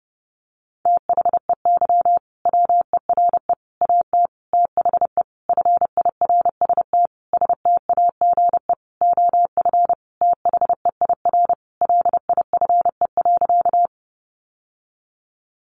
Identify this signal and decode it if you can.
Morse code, heard as a keyed tone, about 30 wpm, 700 Hz. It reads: THEY WERE AT THE FIRST STAGE OF THEIR LIFE.
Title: Sentences from Top 1200 Words 30wpm